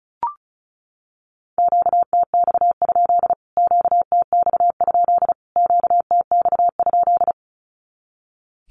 Morse code, heard as a keyed tone, about 34 wpm, 700 Hz. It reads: QTX? QTX? QTX?
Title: Q-Codes Repeated 3x 35wpm